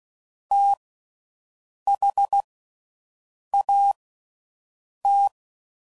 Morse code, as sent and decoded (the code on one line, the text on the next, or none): - .... .- -
THAT